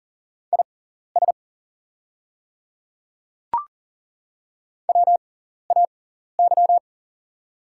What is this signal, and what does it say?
Morse code, heard as a keyed tone, about 40 wpm, spaced as for 14 wpm, 690 Hz. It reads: IS WAY
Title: 6x Character Spacing - Top 100 Words - 40wpm